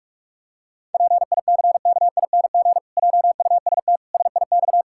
Morse code, heard as a keyed tone, about 46 wpm, 700 Hz. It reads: PICKING JUST SIX
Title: Pangram Challenge v2 - 45wpm